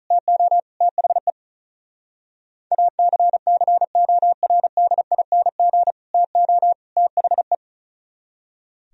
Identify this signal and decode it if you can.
Morse code, heard as a keyed tone, about 46 wpm, 700 Hz. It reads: TO THE ACCORDING TO THE